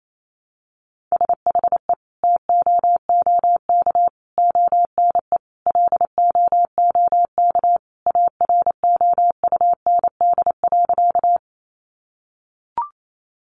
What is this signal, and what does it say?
Morse code, heard as a keyed tone, about 28 wpm, 700 Hz. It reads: SHE TOOK ONE LOOK AROUND.
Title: Sentences from Top 600 Words 28wpm